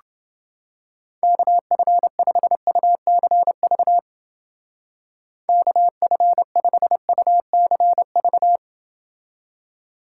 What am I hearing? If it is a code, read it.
Morse code, heard as a keyed tone, about 29 wpm, 700 Hz. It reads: KF5UCV KF5UCV